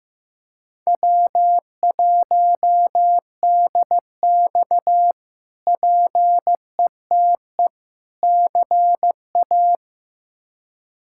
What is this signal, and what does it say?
Morse code, heard as a keyed tone, about 15 wpm, 700 Hz. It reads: W1DX PETE CA
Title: K1USN SST QSO Practice - 15wpm